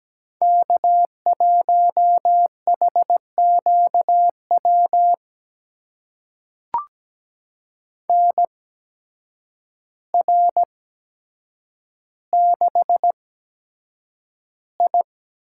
Morse code, heard as a keyed tone, about 17 wpm, spaced as for 8 wpm, 700 Hz. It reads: K1HQW NR6I
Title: USA Call Signs - 8x Character Spacing - 17wpm